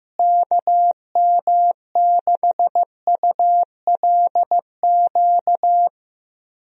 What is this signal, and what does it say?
Morse code, heard as a keyed tone, about 15 wpm, 700 Hz. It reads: KM6ULQ
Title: USA Call Signs - 6 Characters - Repeated 3x - 15wpm